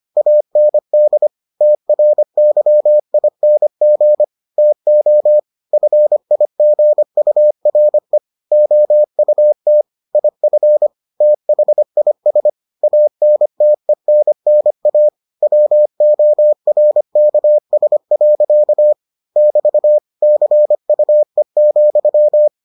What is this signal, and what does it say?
Morse code, heard as a keyed tone, about 25 wpm, 600 Hz. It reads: AND TRYING TO FIGURE OUT IF THIS ANTENNA WORKS. <BT> CUE,